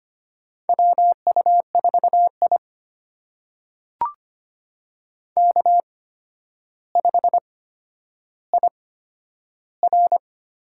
Morse code, heard as a keyed tone, about 25 wpm, spaced as for 11 wpm, 700 Hz. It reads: WU4I K5IR
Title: USA Call Signs - 4 Characters - 8x Character Spacing - 25wpm